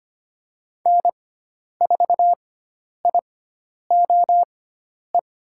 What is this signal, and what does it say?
Morse code, heard as a keyed tone, about 25 wpm, spaced as for 10 wpm, 700 Hz. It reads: N4IOE
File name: Lesson 16g - Letter c - Call Signs 25wpm